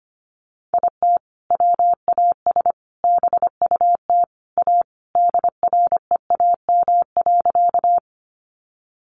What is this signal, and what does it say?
Morse code, heard as a keyed tone, about 25 wpm, 700 Hz. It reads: IT WAS BUT A DREAM.
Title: Sentences from Top 1500 Words 25wpm